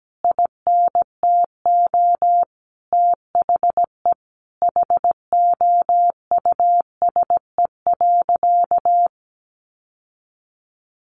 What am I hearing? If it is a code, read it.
Morse code, heard as a keyed tone, about 17 wpm, 700 Hz. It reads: INTO THE HOUSE.
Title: Sentences from Top 600 Words 17wpm